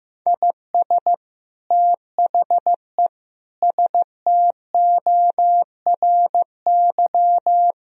Morse code, heard as a keyed tone, about 15 wpm, 700 Hz. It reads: IS THE STORY
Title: Sentences from Top 1000 Words - Review Part 1 - 15wpm